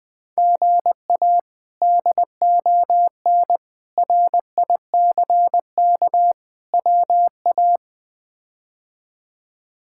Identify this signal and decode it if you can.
Morse code, heard as a keyed tone, about 20 wpm, 700 Hz. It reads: GA DON RICK WA
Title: K1USN SST QSO Practice - 20wpm